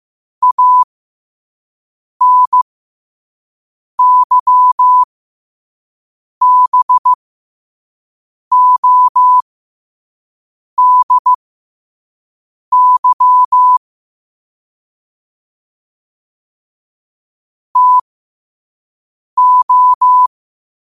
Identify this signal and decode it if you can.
Morse code, heard as a keyed tone, about 15 wpm, spaced as for 5 wpm, 1000 Hz. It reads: ANYBODY TO
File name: qotdfiles / 5wpm